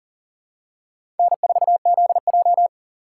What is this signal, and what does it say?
Morse code, heard as a keyed tone, about 39 wpm, 700 Hz. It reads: N4ZJ
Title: USA Call Signs - 4 Characters - 6x Character Spacing - 40wpm